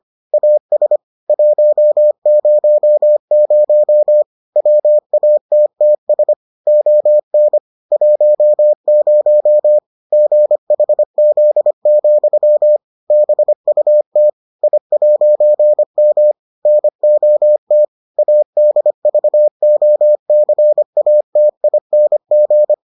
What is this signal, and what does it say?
Morse code, heard as a keyed tone, about 25 wpm, 600 Hz. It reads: AS 100 WATTS ON 10 GHZ, BUT I'M NOT ADVOCATING